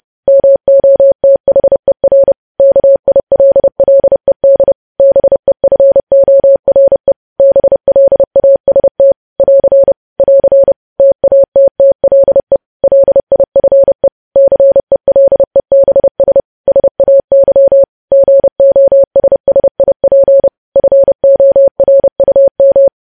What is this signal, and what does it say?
Morse code, heard as a keyed tone, about 30 wpm, 550 Hz. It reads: MOTHER KILLED BEFORE BLAST <AR> <AR> TATTLE LIFE CELEBS SAY GOSSIP FORUM
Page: news headlines in morse code at 30 wpm